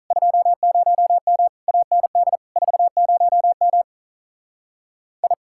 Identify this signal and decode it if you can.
Morse code, heard as a keyed tone, about 44 wpm, 700 Hz. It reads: 20M AND 40M I
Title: Speed-Racing Common QSO Phrases Encore - 35wpm